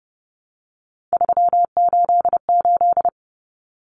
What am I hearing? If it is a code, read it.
Morse code, heard as a keyed tone, about 30 wpm, 700 Hz. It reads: 388
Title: Common Cut Numbers - 3-Digit 30wpm